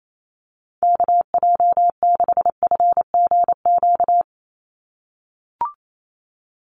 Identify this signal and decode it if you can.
Morse code, heard as a keyed tone, about 28 wpm, 700 Hz. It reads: KJ6FGQ
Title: USA Call Signs 28wpm